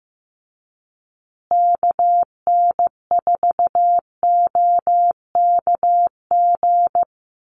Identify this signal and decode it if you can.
Morse code, heard as a keyed tone, about 15 wpm, 700 Hz. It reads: KN4OKG
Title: USA Call Signs Repeated 3x 15wpm